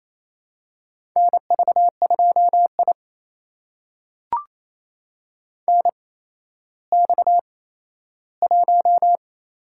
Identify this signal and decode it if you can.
Morse code, heard as a keyed tone, about 28 wpm, spaced as for 14 wpm, 700 Hz. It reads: NV2I NX1
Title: USA Call Signs - 4 Characters - 8x Character Spacing - 28wpm